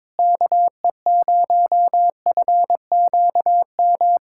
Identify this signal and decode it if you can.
Morse code, heard as a keyed tone, about 22 wpm, 700 Hz. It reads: KE0FQM
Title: USA Callsign DE Callsign 22wpm